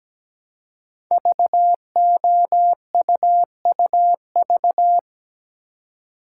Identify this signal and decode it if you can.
Morse code, heard as a keyed tone, about 17 wpm, 700 Hz. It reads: VOUUV
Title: Code Groups - 5 Letters - 17wpm